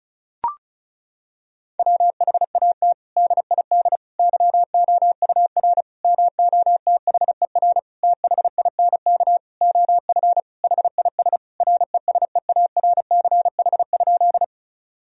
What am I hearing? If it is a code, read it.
Morse code, heard as a keyed tone, about 34 wpm, 700 Hz. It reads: WHAT DID YOUR MOTHER THINK OF HIS RESEARCH?
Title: Sentences from Top 1000 Words - Review Part 4 - 35wpm